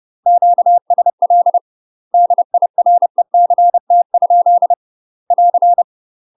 Morse code, heard as a keyed tone, about 30 wpm, 700 Hz. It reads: QSL DIRECT? <AR>
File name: UK MASTER QSOs TRAINING / 30WPM / UK MASTER QSOs 0038